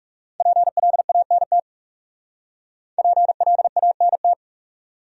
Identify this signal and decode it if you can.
Morse code, heard as a keyed tone, about 35 wpm, 700 Hz. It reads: PLANT PLANT
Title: Speed-Racing Top 500 Words 30wpm